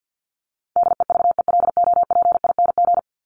Morse code, heard as a keyed tone, about 50 wpm, 700 Hz. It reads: DEVELOPING
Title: ING Suffix - Top 100 Words with Base Words 50wpm